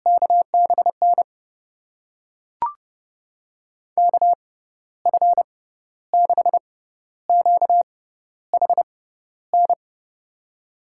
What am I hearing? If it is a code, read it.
Morse code, heard as a keyed tone, about 32 wpm, spaced as for 13 wpm, 700 Hz. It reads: KBN KF6QHN
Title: USA Call Signs - 6 Characters - 6x Character Spacing - 30wpm